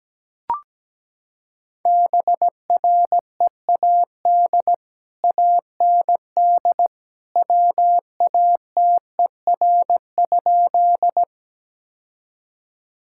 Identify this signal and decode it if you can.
Morse code, heard as a keyed tone, about 17 wpm, 700 Hz. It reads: BREAD AND WATER?
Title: Sentences from Top 1600 Words 17wpm